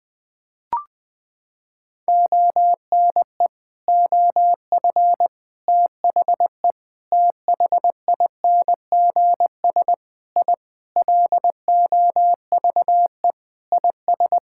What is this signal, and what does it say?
Morse code, heard as a keyed tone, about 20 wpm, 700 Hz. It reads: ONE OF THE THINGS I LOVE IS
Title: Sentences with 4-Word Phrases - 20wpm